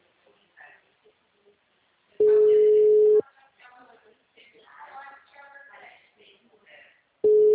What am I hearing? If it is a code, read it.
Ringback tone (Europe, 425 Hz), with speech.
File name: out- -2001- - - .43408